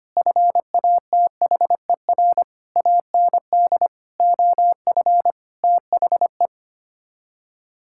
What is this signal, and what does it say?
Morse code, heard as a keyed tone, about 25 wpm, 700 Hz. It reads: FATHER AND OF THE
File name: Sets of 9 Words - 25wpm